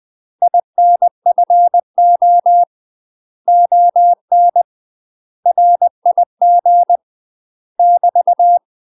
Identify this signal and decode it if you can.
Morse code, heard as a keyed tone, about 20 wpm, 700 Hz. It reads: INFO ON RIG <BT>
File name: UK MASTER QSOs TRAINING / 20WPM / UK MASTER QSOs 0023